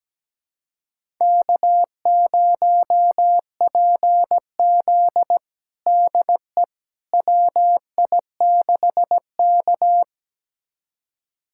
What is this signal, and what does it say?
Morse code, heard as a keyed tone, about 17 wpm, 700 Hz. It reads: K0PZ DE WI6K
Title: USA Callsign DE Callsign - 4 Characters - 17wpm